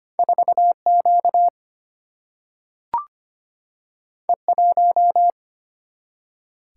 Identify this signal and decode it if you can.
Morse code, heard as a keyed tone, about 25 wpm, 700 Hz. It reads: 4 Q E 1
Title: Single-Letter Single-Number 25wpm